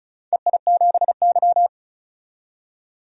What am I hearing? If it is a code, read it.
Morse code, heard as a keyed tone, about 34 wpm, 700 Hz. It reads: EI7Y
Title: DXCC-100 Call Signs - 4 Characters - 35wpm